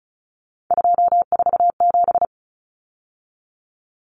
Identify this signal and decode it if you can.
Morse code, heard as a keyed tone, about 34 wpm, 700 Hz. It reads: 247
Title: Numbers - 3-Digit 35wpm